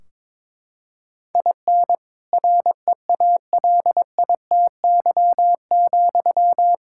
Morse code, heard as a keyed tone, about 22 wpm, 700 Hz. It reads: IN REALITY,
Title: Sentences from Top 1600 Words 22wpm